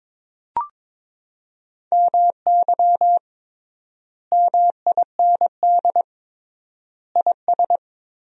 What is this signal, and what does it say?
Morse code, heard as a keyed tone, about 22 wpm, 700 Hz. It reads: MY MIND IS
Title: 3x Word-Spacing - Sentences from Top 500 Words - 22wpm